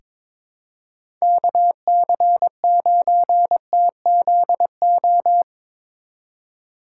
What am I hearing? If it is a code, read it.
Morse code, heard as a keyed tone, about 22 wpm, 700 Hz. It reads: KC9TZO
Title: K1USN SST QSO Practice - 22wpm